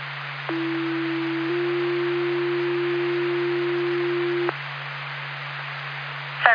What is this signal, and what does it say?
Two-tone page, heard as a one-second tone, then a three-second tone, about 330 then 350 Hz, with speech. It is heard from 0.5 to 4.5 s.